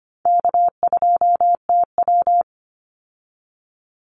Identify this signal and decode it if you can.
Morse code, heard as a keyed tone, about 25 wpm, 700 Hz. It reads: K2TW
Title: Easy Call Signs - 25wpm